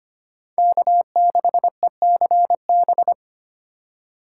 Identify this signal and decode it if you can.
Morse code, heard as a keyed tone, about 32 wpm, 700 Hz. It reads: K6ECB